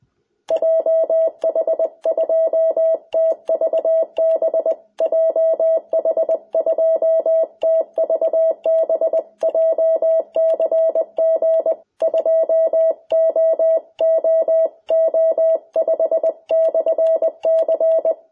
Morse code, heard as a keyed tone, about 20 wpm, 660 Hz. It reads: JH2TVB JH2TVB JCG 2OOO5/C
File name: jh8cbh / 2021cw